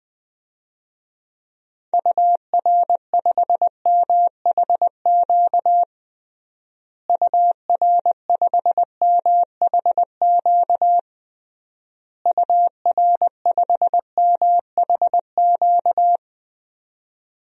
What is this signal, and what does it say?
Morse code, heard as a keyed tone, about 20 wpm, 700 Hz. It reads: UR5MHQ UR5MHQ UR5MHQ
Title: DXCC-100 Call Signs - 6 Characters - Repeated 3x - 20wpm